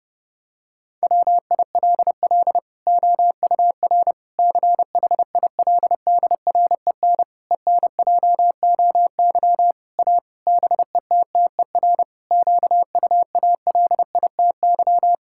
Morse code, heard as a keyed tone, about 32 wpm, 700 Hz. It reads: WILL OUR CHILDREN ENJOY A BETTER QUALITY
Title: Sentences from Top 1000 Words - Review Part 5 - 30wpm